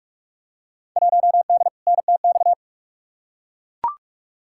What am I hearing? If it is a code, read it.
Morse code, heard as a keyed tone, about 43 wpm, 700 Hz. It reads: 1D NTX
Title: Field Day Exchanges - 45wpm